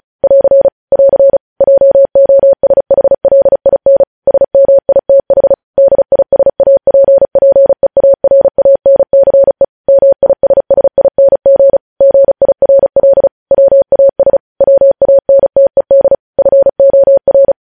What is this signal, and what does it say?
Morse code, heard as a keyed tone, about 36 wpm, 550 Hz. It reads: <AR> <AR> JOSHLIN SMITH DISAPPEARANCE MISSING GIRL WAS WANTED FOR
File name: podcast / News headlines-1x-35x35